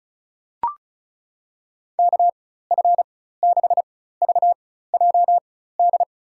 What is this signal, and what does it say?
Morse code, heard as a keyed tone, about 34 wpm, spaced as for 16 wpm, 700 Hz. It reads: KF6VJD
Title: USA Call Signs - 6 Characters - 4x Character Spacing - 35wpm